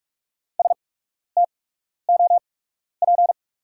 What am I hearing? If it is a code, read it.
Morse code, heard as a keyed tone, about 46 wpm, spaced as for 12 wpm, 690 Hz. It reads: STOP
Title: 8x Character Spacing - Top 300 Words - 45wpm